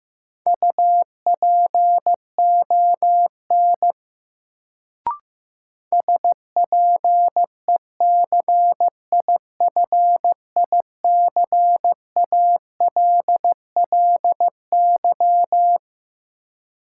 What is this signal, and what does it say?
Morse code, heard as a keyed tone, about 15 wpm, 700 Hz. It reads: UPON SPECIFICALLY
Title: Sentences from Top 1100 Words - New Word Drill Alt - 15wpm